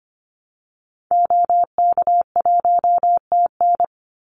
Morse code, heard as a keyed tone, about 25 wpm, 700 Hz. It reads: OK1TN
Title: Call Signs Repeated 2x - 25wpm